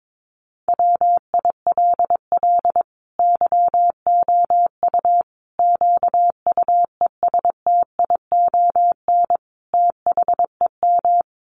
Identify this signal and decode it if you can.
Morse code, heard as a keyed tone, about 22 wpm, 700 Hz. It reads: WILL YOU QUESTION THEM
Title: Sentences from Top 1100 Words 22wpm